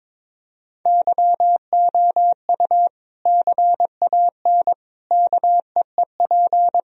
Morse code, heard as a keyed tone, about 22 wpm, 700 Hz. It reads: YOU CAN KEEP
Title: Sentences from Top 1000 Words - Review Part 1 - 22wpm